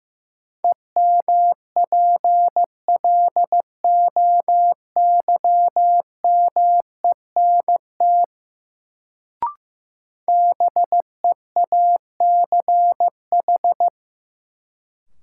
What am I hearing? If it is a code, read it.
Morse code, heard as a keyed tone, about 15 wpm, 700 Hz. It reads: EMPLOYMENT BEACH
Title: Sentences from Top 1500 Words - New Word Drill - 15wpm